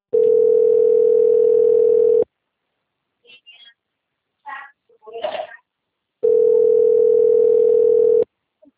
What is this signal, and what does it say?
Ringback tone (North America), 440 + 480 Hz.